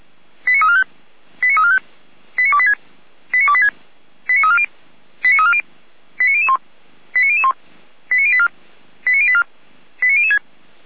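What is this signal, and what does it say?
Five-tone selective call (ZVEI), heard as quick runs of five tones, tones of about 80 ms. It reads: 89345 89345 89286 89286 89340 89340 89001 89001 89084 89084 89006
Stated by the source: ZVEI five-tone selective call received on SDR.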